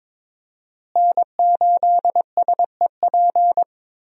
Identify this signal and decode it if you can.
Morse code, heard as a keyed tone, about 22 wpm, 700 Hz. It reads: N8SEP
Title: USA Call Signs - 5 Characters - Repeated 3x - 22wpm